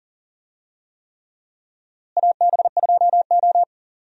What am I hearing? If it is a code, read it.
Morse code, heard as a keyed tone, about 40 wpm, 700 Hz. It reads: AB2O
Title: USA Call Signs - 4 Characters - Repeated 3x - 40wpm